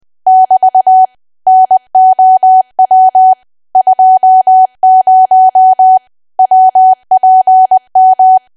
Morse code, heard as a keyed tone, about 20 wpm, 750 Hz. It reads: <BT> NOW 20 WPM
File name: cwa / W1AW-Level3 / 20WPM